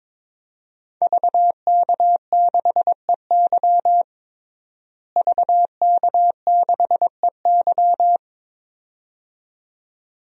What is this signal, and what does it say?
Morse code, heard as a keyed tone, about 21 wpm, 700 Hz. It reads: VK6EY VK6EY
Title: DXCC-100 Call Signs - 5 Characters - Repeated 2x - 22wpm